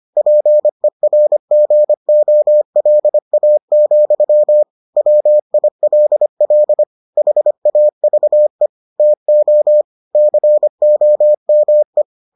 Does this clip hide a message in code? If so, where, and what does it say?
Yes: Morse code from 0.2 to 12.0 s, heard as a keyed tone, about 25 wpm, 600 Hz. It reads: PERGOLA, WILL HAVE TO COME